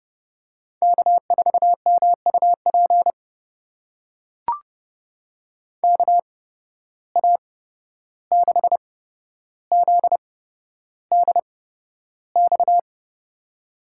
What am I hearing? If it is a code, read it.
Morse code, heard as a keyed tone, about 30 wpm, spaced as for 12 wpm, 700 Hz. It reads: K4MUP KA6ZDX
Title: USA Call Signs - 8x Character Spacing - 30wpm